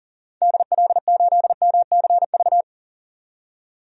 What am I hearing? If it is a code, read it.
Morse code, heard as a keyed tone, about 39 wpm, 700 Hz. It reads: DL8MCV
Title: DXCC-100 Call Signs - 6 Characters - 40wpm